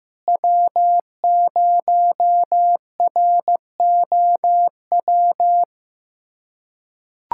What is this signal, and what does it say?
Morse code, heard as a keyed tone, about 16 wpm, 700 Hz. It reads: W0ROW
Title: USA Callsign DE Callsign - 5 Characters - 15wpm